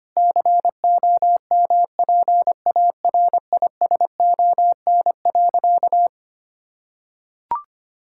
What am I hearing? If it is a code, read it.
Morse code, heard as a keyed tone, about 25 wpm, 700 Hz. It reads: COMPARISON.
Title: Sentences from Top 1600 Words 25wpm